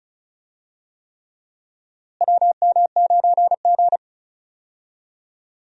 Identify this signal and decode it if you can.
Morse code, heard as a keyed tone, about 36 wpm, 700 Hz. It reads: WM9G
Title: USA Call Signs - 4 Characters - Repeated 3x - 35wpm